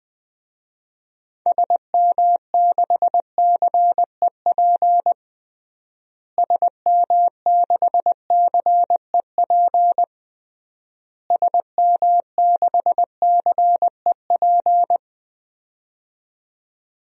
Morse code, heard as a keyed tone, about 20 wpm, 700 Hz. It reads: SM6CEP SM6CEP SM6CEP
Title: DXCC-100 Call Signs - 6 Characters - Repeated 3x - 20wpm